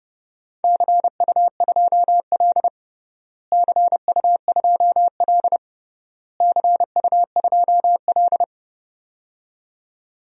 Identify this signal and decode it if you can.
Morse code, heard as a keyed tone, about 30 wpm, 700 Hz. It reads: CU2L CU2L CU2L
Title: DXCC-100 Call Signs - 4 Characters - Repeated 3x - 30wpm